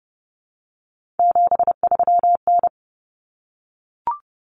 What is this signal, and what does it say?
Morse code, heard as a keyed tone, about 30 wpm, 700 Hz. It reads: 73N
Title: Common Cut Numbers - 3-Digit 30wpm